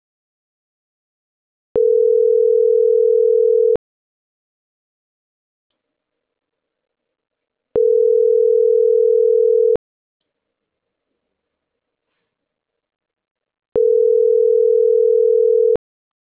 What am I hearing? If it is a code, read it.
Ringback tone (North America), 440 + 480 Hz.
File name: CENTRAL NAVIERAS / out onelinecapacitacion - .76130